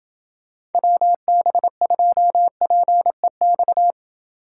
Morse code, heard as a keyed tone, about 28 wpm, 700 Hz. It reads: WB2PEX